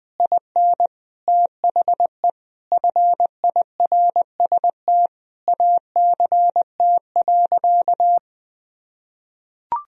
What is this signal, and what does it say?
Morse code, heard as a keyed tone, about 20 wpm, 700 Hz. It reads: IN THE FIRST ACT.
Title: Sentences from Top 1000 Words - Review Part 2 - 20wpm